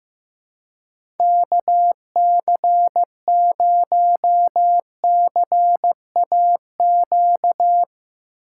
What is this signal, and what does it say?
Morse code, heard as a keyed tone, about 15 wpm, 700 Hz. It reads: KC0CAQ
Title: USA Call Signs - 6 Characters - 4x Character Spacing - 15wpm